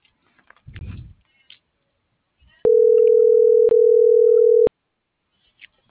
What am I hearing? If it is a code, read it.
Ringback tone (North America), 440 + 480 Hz.